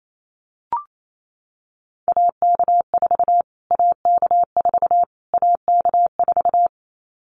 Morse code, heard as a keyed tone, about 28 wpm, 700 Hz. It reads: AK4 AK4 AK4
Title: Callsign Prefixes for Top 25 DXCC - Repeated 3x - 28wpm